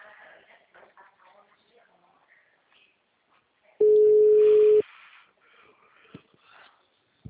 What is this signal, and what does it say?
Ringback tone (Europe, 425 Hz), with speech.